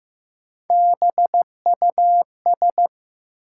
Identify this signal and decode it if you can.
Morse code, heard as a keyed tone, about 15 wpm, 700 Hz. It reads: BUS
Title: Sentences from Top 1000 Words - Review Part 6 - New Word Drill - 15wpm